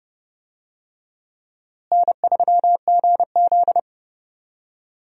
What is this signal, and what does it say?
Morse code, heard as a keyed tone, about 30 wpm, 700 Hz. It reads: N3GZ
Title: USA Call Signs - 4 Characters - Repeated 3x - 30wpm